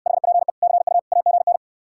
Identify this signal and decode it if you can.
Morse code, heard as a keyed tone, about 69 wpm, 700 Hz. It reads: HPE CU AGN